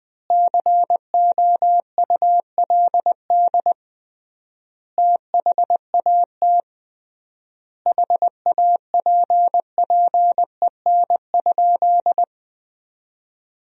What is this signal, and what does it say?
Morse code, heard as a keyed tone, about 20 wpm, 700 Hz. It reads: COULD THAT HAPPEN?
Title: 3x Word-Spacing - Sentences from Top 400 Words - 20wpm